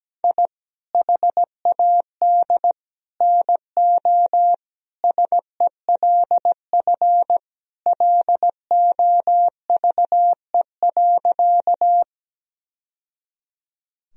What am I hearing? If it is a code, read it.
Morse code, heard as a keyed tone, about 17 wpm, 700 Hz. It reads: I HAD NO SELF LOVE.
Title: Sentences from Top 1000 Words - Review Part 1 - 17wpm